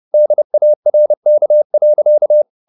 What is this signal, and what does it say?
Morse code, heard as a keyed tone, about 30 wpm, 600 Hz. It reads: DARK.
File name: the old man and the sea / the old man and the sea 1 30wpm